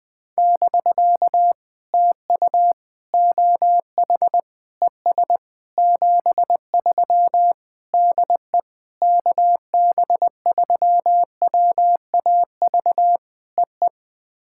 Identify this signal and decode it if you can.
Morse code, heard as a keyed tone, about 20 wpm, 700 Hz. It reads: <BK> TU OH ES 73 DE KB3WAV EE
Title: POTA QSO Practice - 20wpm